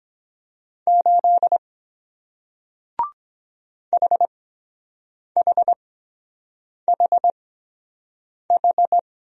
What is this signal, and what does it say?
Morse code, heard as a keyed tone, about 23 wpm, spaced as for 7 wpm, 700 Hz. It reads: 8 HHHH